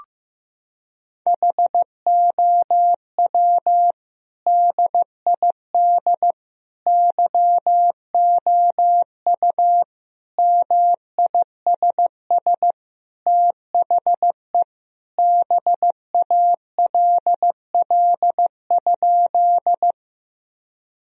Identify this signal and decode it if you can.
Morse code, heard as a keyed tone, about 15 wpm, 700 Hz. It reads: HOW DID YOU MISS THE BALL?
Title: Sentences from Top 1000 Words - Review Part 3 - 15wpm